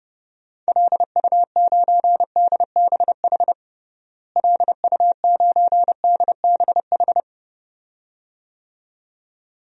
Morse code, heard as a keyed tone, about 30 wpm, 700 Hz. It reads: LU9DBH LU9DBH
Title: DXCC-100 Call Signs - 6 Characters - Repeated 2x - 30wpm